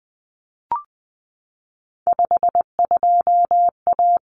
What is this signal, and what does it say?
Morse code, heard as a keyed tone, about 20 wpm, 700 Hz. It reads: 52A
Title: Common Cut Numbers - 3-Digit 20wpm